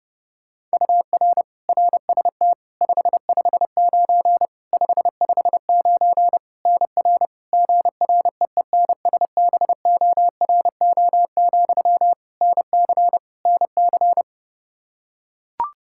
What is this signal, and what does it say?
Morse code, heard as a keyed tone, about 32 wpm, 700 Hz. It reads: UR RST 559 559 NR GREENSBORO, NC NC
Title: K1USN SST QSO Practice 30wpm